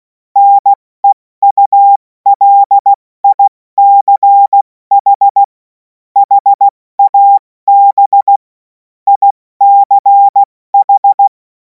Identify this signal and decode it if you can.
Morse code, heard as a keyed tone, about 16 wpm, 800 Hz. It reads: NEULICH HAB ICH
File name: Morsekurs / Witze / Witz 11. August 2026 70 F80